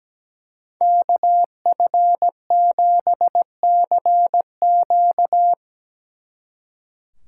Morse code, heard as a keyed tone, about 20 wpm, 700 Hz. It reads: KF7CQ
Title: Speed Racing - USA Call Signs 17wpm